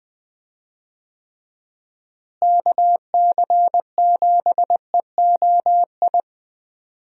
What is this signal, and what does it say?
Morse code, heard as a keyed tone, about 20 wpm, 700 Hz. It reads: KC7EOI
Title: USA Call Signs - 6 Characters - Repeated 2x - 20wpm